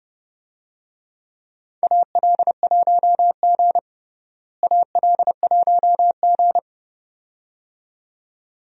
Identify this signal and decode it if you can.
Morse code, heard as a keyed tone, about 30 wpm, 700 Hz. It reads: AL1G AL1G
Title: USA Call Signs - 4 Characters - Repeated 2x - 30wpm